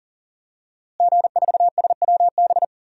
Morse code, heard as a keyed tone, about 40 wpm, 700 Hz. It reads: G4SWB
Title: DXCC-100 Call Signs - 5 Characters - Repeated 2x - 40wpm